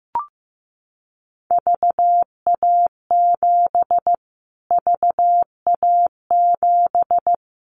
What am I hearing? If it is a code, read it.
Morse code, heard as a keyed tone, about 15 wpm, 700 Hz. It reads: VA7 VA7
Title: Callsign Prefixes for Top 25 DXCC - Repeated 3x - 15wpm